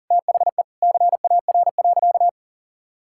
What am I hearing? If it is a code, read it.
Morse code, heard as a keyed tone, about 40 wpm, 700 Hz. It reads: THE CAR.
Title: Sentences from Top 1000 Words - Review Part 5 - 40wpm